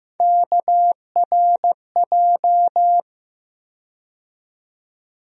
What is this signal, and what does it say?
Morse code, heard as a keyed tone, about 15 wpm, 700 Hz. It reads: KRJ
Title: USA Call Sign Postfixes - 3 Letters 15wpm